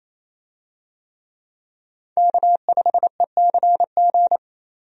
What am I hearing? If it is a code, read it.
Morse code, heard as a keyed tone, about 28 wpm, 700 Hz. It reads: K5ECG
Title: USA Call Signs - 5 Characters - 28wpm